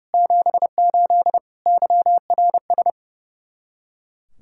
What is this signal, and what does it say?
Morse code, heard as a keyed tone, about 40 wpm, 700 Hz. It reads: 78 YRS